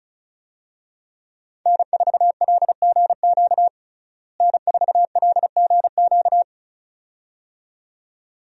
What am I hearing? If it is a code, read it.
Morse code, heard as a keyed tone, about 36 wpm, 700 Hz. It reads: N4LGQ N4LGQ
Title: USA Call Signs - 5 Characters - Repeated 2x - 35wpm